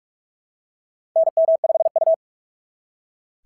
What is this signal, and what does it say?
Morse code, heard as a keyed tone, about 44 wpm, 690 Hz. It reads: NM5U
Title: SOTA QSO Practice 45wpm